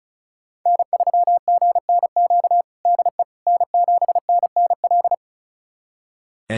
Morse code, heard as a keyed tone, about 36 wpm, 700 Hz. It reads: N3GNQ DE N7NNL
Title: USA Callsign DE Callsign - 5 Characters - 35wpm